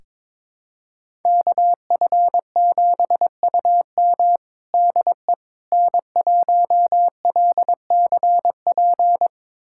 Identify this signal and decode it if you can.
Morse code, heard as a keyed tone, about 22 wpm, 700 Hz. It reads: KF7UM DE N1LCP
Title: USA Callsign DE Callsign - 5 Characters - 22wpm